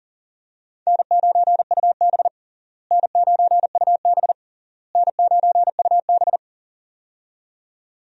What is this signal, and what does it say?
Morse code, heard as a keyed tone, about 40 wpm, 700 Hz. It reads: N9UB N9UB N9UB
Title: USA Call Signs - 4 Characters - Repeated 3x - 40wpm